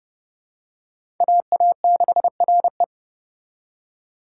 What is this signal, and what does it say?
Morse code, heard as a keyed tone, about 32 wpm, 700 Hz. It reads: AA6RE
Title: K1USN SST QSO Practice - 30wpm